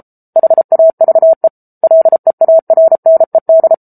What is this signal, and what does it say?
Morse code, heard as a keyed tone, about 34 wpm, 650 Hz. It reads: HAVE LEARNED
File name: Book Two / 07-Chapter 7 - 33 WPM